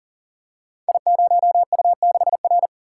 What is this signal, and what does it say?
Morse code, heard as a keyed tone, about 38 wpm, 700 Hz. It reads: I0U6R
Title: Code Groups - 5 Letter:Numbers - 40wpm